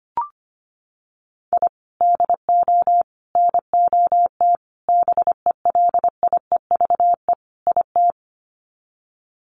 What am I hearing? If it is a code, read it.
Morse code, heard as a keyed tone, about 25 wpm, 700 Hz. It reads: I DO NOT BELIEVE IT
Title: Sets of 5 Words 25wpm